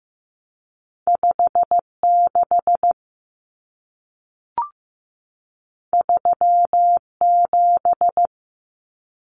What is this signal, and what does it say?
Morse code, heard as a keyed tone, about 15 wpm, 700 Hz. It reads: 56 37
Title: Numbers - 2-Digit 15wpm